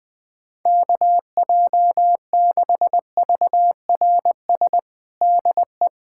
Morse code, heard as a keyed tone, about 20 wpm, 700 Hz. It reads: KJ6VRS DE
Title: USA Callsign DE Callsign - 6 Characters - 20wpm